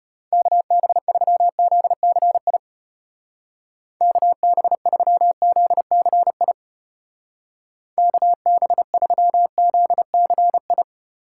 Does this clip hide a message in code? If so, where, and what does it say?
Yes: Morse code from 0.3 to 10.8 s, heard as a keyed tone, about 32 wpm, 700 Hz. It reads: KB3ZCI KB3ZCI KB3ZCI